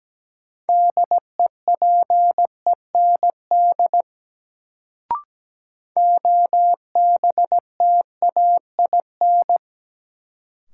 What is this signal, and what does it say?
Morse code, heard as a keyed tone, about 17 wpm, 700 Hz. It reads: DEPEND OBTAIN
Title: Sentences from Top 1300 Words - New Word Drill - 17wpm